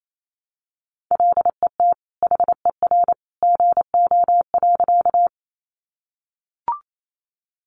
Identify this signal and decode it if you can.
Morse code, heard as a keyed tone, about 28 wpm, 700 Hz. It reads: LET HER GO.
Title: Sentences from Top 300 Words 28wpm